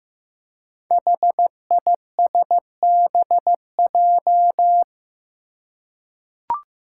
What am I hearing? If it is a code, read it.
Morse code, heard as a keyed tone, about 15 wpm, 700 Hz. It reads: HISBJ
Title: Five Challenging Letter-Number Combinations 15wpm